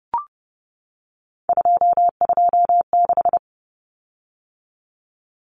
Morse code, heard as a keyed tone, about 30 wpm, 700 Hz. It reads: 226
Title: Common Cut Numbers - 3-Digit 30wpm